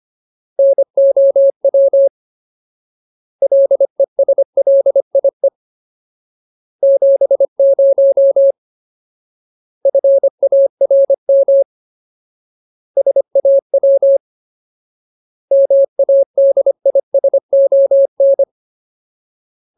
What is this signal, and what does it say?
Morse code, heard as a keyed tone, about 25 wpm, 550 Hz. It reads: NOW LESLIE 70 FARM SAW MADISON
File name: Ultimate Word List / 25WPM Ultimate